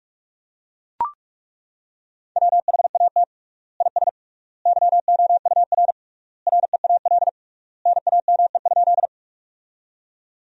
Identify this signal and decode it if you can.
Morse code, heard as a keyed tone, about 43 wpm, 700 Hz. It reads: WHAT IS YOUR REAL NAME?
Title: 3x Word-Spacing - Sentences from Top 300 Words - 45wpm